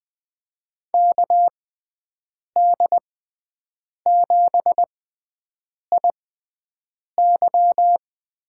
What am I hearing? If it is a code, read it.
Morse code, heard as a keyed tone, about 20 wpm, spaced as for 7 wpm, 700 Hz. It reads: KD7IY
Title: USA Call Signs - 6 Characters - 6x Character Spacing - 20wpm